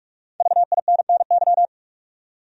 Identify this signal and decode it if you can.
Morse code, heard as a keyed tone, about 46 wpm, 700 Hz. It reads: VINNY
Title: USA Common Names - Max 5 Letters - 45wpm